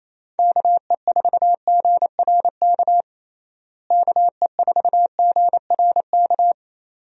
Morse code, heard as a keyed tone, about 28 wpm, 700 Hz. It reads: KE4GRK KE4GRK
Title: USA Call Signs - 6 Characters - Repeated 2x - 28wpm